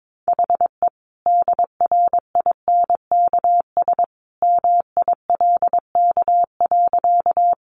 Morse code, heard as a keyed tone, about 22 wpm, 700 Hz. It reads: HE DRINKS MILK.
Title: Easy Phrases 22wpm